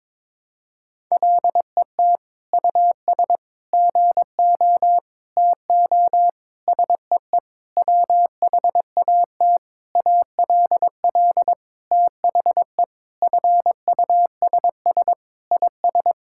Morse code, heard as a keyed tone, about 21 wpm, 700 Hz. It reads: LET US GO TO SEE WHAT ALL THE FUSS IS
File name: practice / 22wpm / Sentences with 8-Word Phrases - 22wpm